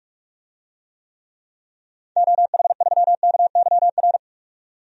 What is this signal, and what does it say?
Morse code, heard as a keyed tone, about 43 wpm, 700 Hz. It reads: OH3KYR
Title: DXCC-100 Call Signs - 6 Characters - 45wpm